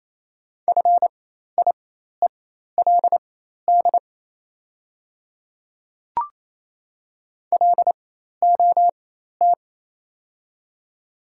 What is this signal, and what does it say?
Morse code, heard as a keyed tone, about 28 wpm, spaced as for 13 wpm, 700 Hz. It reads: FIELD LOT
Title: 4x Character Spacing - Top 500 Words - 28wpm